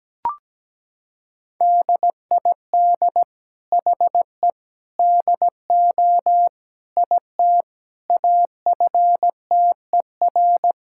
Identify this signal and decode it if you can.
Morse code, heard as a keyed tone, about 17 wpm, 700 Hz. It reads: DID HE DO IT AFTER
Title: Sentences from Top 1600 Words 17wpm